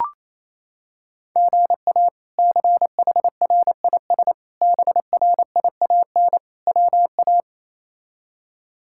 Morse code, heard as a keyed tone, about 28 wpm, 700 Hz. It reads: GA CHRIS BRIAN WA
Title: K1USN SST QSO Practice - 28wpm